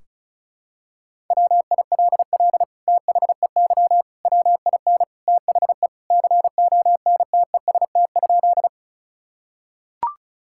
Morse code, heard as a keyed tone, about 36 wpm, 700 Hz. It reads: WILL THEY WIN THE CONTEST?
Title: Sentences from Top 1600 Words 35wpm